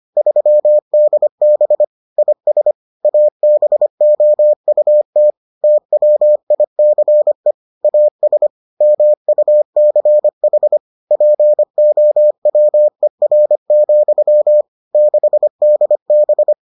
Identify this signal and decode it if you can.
Morse code, heard as a keyed tone, about 25 wpm, 600 Hz. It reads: 3DB IS ABOUT TWICE AS MUCH POWER, 6DB